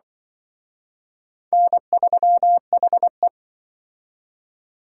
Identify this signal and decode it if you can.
Morse code, heard as a keyed tone, about 24 wpm, 700 Hz. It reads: N3HE
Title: Speed Racing - USA 1x2 Call Signs 20wpm